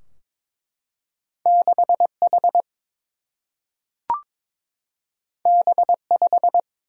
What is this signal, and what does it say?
Morse code, heard as a keyed tone, about 22 wpm, 700 Hz. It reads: 6 H B 5
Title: Single-Letter Single-Number 22wpm